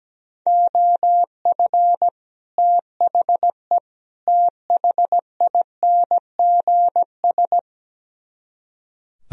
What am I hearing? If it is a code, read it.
Morse code, heard as a keyed tone, about 17 wpm, 700 Hz. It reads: OF THE THINGS
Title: Speed-Racing - Sets of 3 Words 17wpm